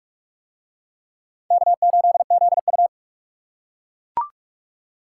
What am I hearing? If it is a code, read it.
Morse code, heard as a keyed tone, about 39 wpm, 700 Hz. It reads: K8ZU